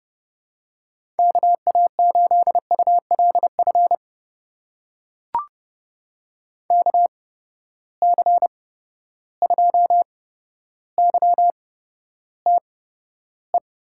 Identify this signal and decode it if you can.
Morse code, heard as a keyed tone, about 30 wpm, spaced as for 13 wpm, 700 Hz. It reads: KA8ULF KC2YTE
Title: USA Call Signs - 6 Characters - 8x Character Spacing - 30wpm